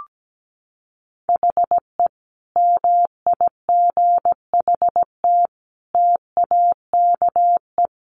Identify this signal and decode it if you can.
Morse code, heard as a keyed tone, about 17 wpm, 700 Hz. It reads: HE MIGHT TAKE
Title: Sentences from Top 1000 Words 17wpm